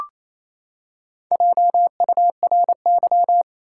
Morse code, heard as a keyed tone, about 28 wpm, 700 Hz. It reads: JURY
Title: Four Letter Words - 28wpm